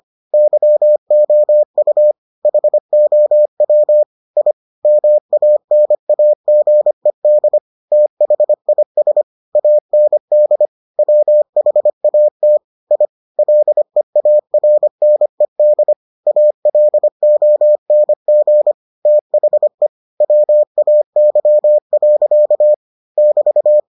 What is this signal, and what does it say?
Morse code, heard as a keyed tone, about 25 wpm, 600 Hz. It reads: YOU HOW I MANAGED THIS AND WHAT I LEARNED ALONG THE WAY. <BT>